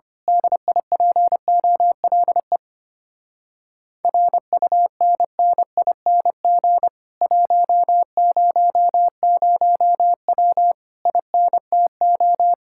Morse code, heard as a keyed tone, about 25 wpm, 700 Hz. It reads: DIPOLE RUNNING 100W INTO
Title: Speed-Racing Common QSO Phrases - 25wpm